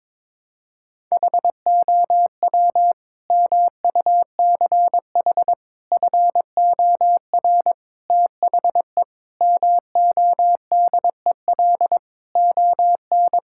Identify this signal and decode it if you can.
Morse code, heard as a keyed tone, about 22 wpm, 700 Hz. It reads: HOW MUCH FOR THE MODEL ON
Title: Sentences from Top 1000 Words - Review Part 5 - 22wpm